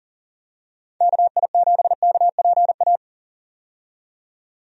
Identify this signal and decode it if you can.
Morse code, heard as a keyed tone, about 38 wpm, 700 Hz. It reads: KI7KPA
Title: USA Call Signs - 6 Characters - 40wpm